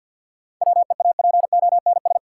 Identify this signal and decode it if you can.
Morse code, heard as a keyed tone, about 52 wpm, 700 Hz. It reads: WEAPONS
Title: Sets of 5 Words Encore - New Word Drill - 50wpm